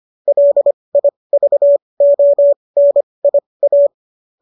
Morse code, heard as a keyed tone, about 25 wpm, spaced as for 20 wpm, 570 Hz. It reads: LIVONIA